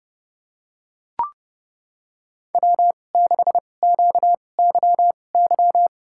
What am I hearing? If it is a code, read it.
Morse code, heard as a keyed tone, about 28 wpm, spaced as for 21 wpm, 690 Hz. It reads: W6QYY
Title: USA Call Signs - 5 Characters - 2x Character Spacing - 30wpm